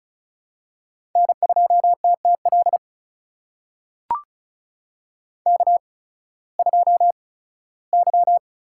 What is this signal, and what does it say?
Morse code, heard as a keyed tone, about 36 wpm, spaced as for 19 wpm, 700 Hz. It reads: N2TTL K2Y
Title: USA Call Signs - 5 Characters - 8x Character Spacing - 35wpm